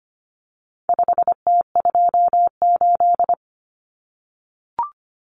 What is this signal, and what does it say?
Morse code, heard as a keyed tone, about 25 wpm, 700 Hz. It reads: 5T28
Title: Common Cut Numbers - 4-Digit 25wpm